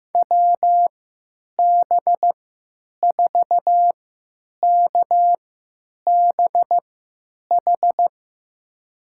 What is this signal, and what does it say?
Morse code, heard as a keyed tone, about 15 wpm, spaced as for 9 wpm, 700 Hz. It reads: WB4KBH
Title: USA Call Signs - 6 Characters - 3x Character Spacing - 15wpm